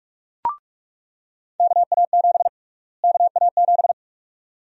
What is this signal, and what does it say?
Morse code, heard as a keyed tone, about 46 wpm, 700 Hz. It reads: KA7 KA7
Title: USA Call Sign Prefixes - Repeated 2x - 45wpm